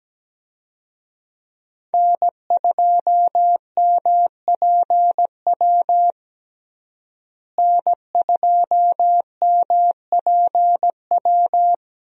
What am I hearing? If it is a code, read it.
Morse code, heard as a keyed tone, about 17 wpm, 700 Hz. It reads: N2MPW N2MPW
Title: USA Call Signs - 5 Characters - Repeated 3x - 17wpm